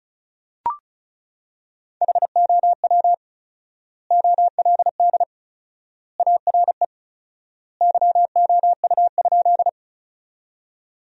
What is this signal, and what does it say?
Morse code, heard as a keyed tone, about 34 wpm, 700 Hz. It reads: HOW OLD ARE YOU?
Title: 4x Word-Spacing - Sentences from Top 200 Words - 35wpm